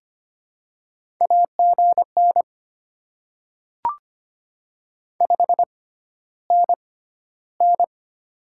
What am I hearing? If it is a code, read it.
Morse code, heard as a keyed tone, about 25 wpm, spaced as for 13 wpm, 700 Hz. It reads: AGN 5NN
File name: Top 20 QSO Elements - 6x Character Spacing - 25wpm